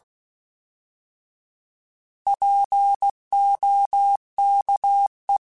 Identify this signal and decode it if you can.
Morse code, heard as a keyed tone, about 16 wpm, 800 Hz. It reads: POKE
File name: Alphabetical / Alphabetical mp3s / Ps @15